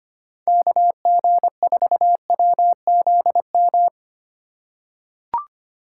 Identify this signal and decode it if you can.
Morse code, heard as a keyed tone, about 25 wpm, 700 Hz. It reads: KG4WZM
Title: USA Call Signs - 6 Characters - 4x Character Spacing - 25wpm